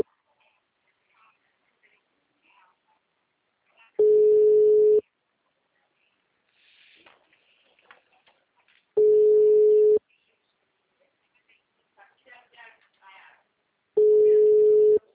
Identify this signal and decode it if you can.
Ringback tone (Europe, 425 Hz), with speech.
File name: out- -2005- - - .46489